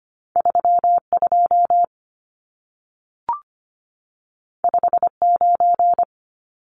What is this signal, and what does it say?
Morse code, heard as a keyed tone, about 25 wpm, 700 Hz. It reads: 32 59
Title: Numbers - 2-Digit 25wpm